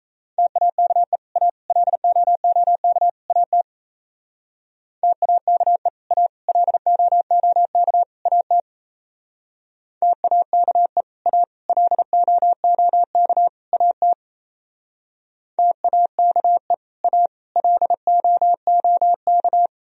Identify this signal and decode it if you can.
Morse code, heard as a keyed tone, about 34 wpm, 700 Hz. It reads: TAKE A LOOK AT TAKE A LOOK AT TAKE A LOOK AT TAKE A LOOK
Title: Speed-Racing - Sets of 4 Words 28wpm